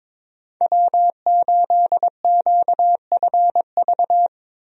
Morse code, heard as a keyed tone, about 22 wpm, 700 Hz. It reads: W8QFV
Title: USA Call Signs - 5 Characters - Repeated 3x - 22wpm